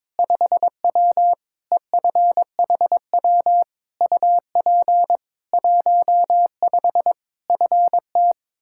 Morse code, heard as a keyed tone, about 22 wpm, 700 Hz. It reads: 5W EFHW UP 15 FT
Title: K1USN SST QSO Practice 22wpm